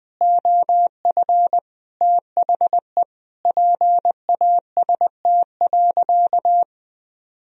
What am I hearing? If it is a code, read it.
Morse code, heard as a keyed tone, about 20 wpm, 700 Hz. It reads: OF THE PAST.
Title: Sentences from Top 1000 Words - Review Part 3 - 20wpm